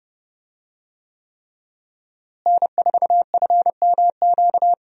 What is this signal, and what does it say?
Morse code, heard as a keyed tone, about 30 wpm, 700 Hz. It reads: N4FMQ
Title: USA Call Signs - 5 Characters - Repeated 3x - 30wpm